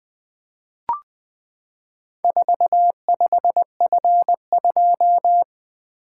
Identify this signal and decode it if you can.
Morse code, heard as a keyed tone, about 20 wpm, 700 Hz. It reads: 45F2
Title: Four Letter-Number Combinations - 20wpm